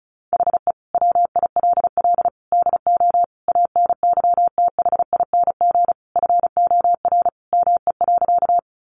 Morse code, heard as a keyed tone, about 34 wpm, 700 Hz. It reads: HE WILL DO ANYTHING FOR ME.
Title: Sentences from Top 1000 Words 35wpm